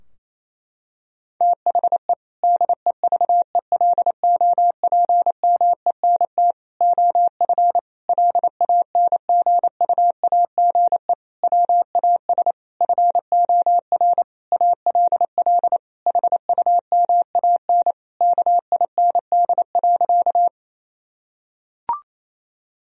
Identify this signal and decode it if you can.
Morse code, heard as a keyed tone, about 28 wpm, 700 Hz. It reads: THE DEVELOPMENT OF LANGUAGE WAS FOR ALL HUMAN KIND.
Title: Sentences from Top 1000 Words - Review Part 4 - 28wpm